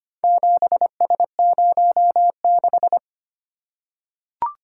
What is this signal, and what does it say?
Morse code, heard as a keyed tone, about 25 wpm, 700 Hz. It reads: 7S06
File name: practice / 25wpm / Four Letter-Number Combinations - 25wpm